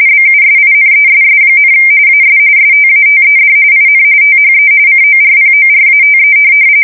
RTTY, about 45.45 baud. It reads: BUSINESS, AS STATED IN THE CORPORATE
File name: fsk170 / teletype-corp